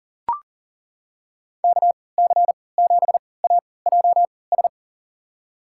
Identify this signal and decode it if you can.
Morse code, heard as a keyed tone, about 40 wpm, spaced as for 23 wpm, 700 Hz. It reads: KC7AJS
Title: USA Call Signs - 3x Character Spacing - 40wpm